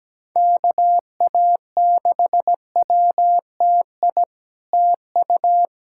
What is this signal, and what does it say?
Morse code, heard as a keyed tone, about 17 wpm, 700 Hz. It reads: KA6WTI TU
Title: SOTA QSO Practice 17wpm